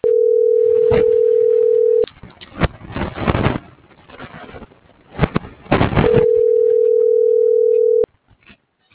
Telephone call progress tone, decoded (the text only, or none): ringback tone